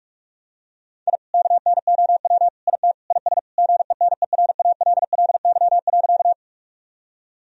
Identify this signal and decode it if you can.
Morse code, heard as a keyed tone, about 41 wpm, 700 Hz. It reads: I KNOW IT IS GENERALLY.
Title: Sentences from Top 1000 Words - Review Part 4 - 45wpm